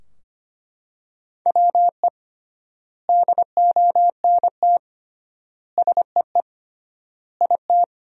Morse code, heard as a keyed tone, about 25 wpm, 700 Hz. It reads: WE DONT SEE IT
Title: 3x Word-Spacing - Sentences from Top 300 Words - 25wpm